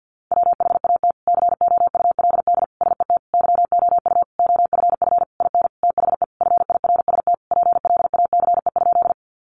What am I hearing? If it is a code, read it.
Morse code, heard as a keyed tone, about 50 wpm, 700 Hz. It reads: WHAT COULD SET YOU OFF IN THE FIRST PLACE?
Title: Sentences from Top 200 Words 50wpm